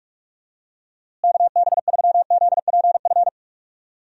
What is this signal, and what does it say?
Morse code, heard as a keyed tone, about 46 wpm, 700 Hz. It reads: KB3ZPF
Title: USA Call Signs - 6 Characters - Repeated 3x - 45wpm